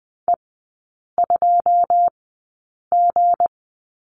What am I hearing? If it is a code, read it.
Morse code, heard as a keyed tone, about 20 wpm, spaced as for 8 wpm, 700 Hz. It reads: E2G
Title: 159 Lesson 26e - Letter g - Three Characters 20wpm